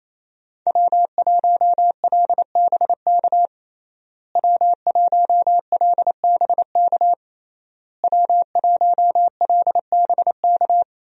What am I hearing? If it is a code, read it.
Morse code, heard as a keyed tone, about 27 wpm, 700 Hz. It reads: W1LBK W1LBK W1LBK
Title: USA Call Signs - 5 Characters - Repeated 3x - 28wpm